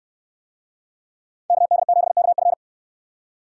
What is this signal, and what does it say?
Morse code, heard as a keyed tone, about 69 wpm, 700 Hz. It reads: KD7KF